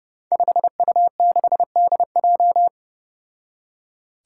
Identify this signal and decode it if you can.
Morse code, heard as a keyed tone, about 30 wpm, 700 Hz. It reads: 5U6DJ
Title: Five Challenging Letter-Number Combinations 30wpm